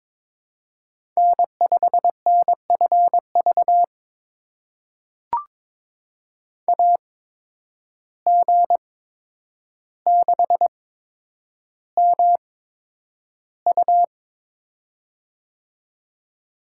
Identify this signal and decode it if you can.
Morse code, heard as a keyed tone, about 22 wpm, spaced as for 9 wpm, 700 Hz. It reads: N5NFV AG6MU
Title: USA Call Signs - 5 Characters - 8x Character Spacing - 22wpm